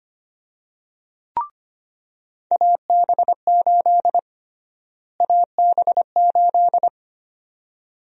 Morse code, heard as a keyed tone, about 25 wpm, 700 Hz. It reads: AB8 AB8
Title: USA Call Sign Prefixes - Repeated 2x - 25wpm